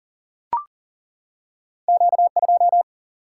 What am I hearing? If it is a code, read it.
Morse code, heard as a keyed tone, about 40 wpm, 700 Hz. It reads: Q2
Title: Single-Letter Single-Number 40wpm